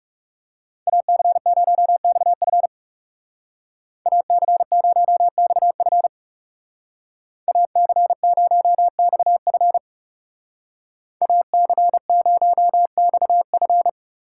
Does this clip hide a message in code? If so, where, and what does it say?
Yes: Morse code from 0.9 to 13.9 s, heard as a keyed tone, about 39 wpm, 700 Hz. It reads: AC0XF AC0XF AC0XF AC0XF